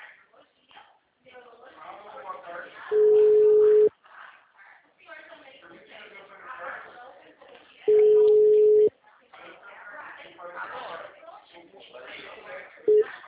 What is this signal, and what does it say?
Ringback tone (Europe, 425 Hz), with speech.